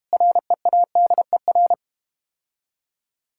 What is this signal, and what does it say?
Morse code, heard as a keyed tone, about 31 wpm, 700 Hz. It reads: READER
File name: practice / 25wpm / Difficult Words - Speed-Racing with Hints - 25wpm